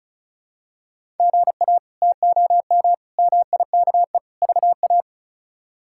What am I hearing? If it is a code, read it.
Morse code, heard as a keyed tone, about 36 wpm, 700 Hz. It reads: GA TOM MIKE VA
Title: K1USN SST QSO Practice - 35wpm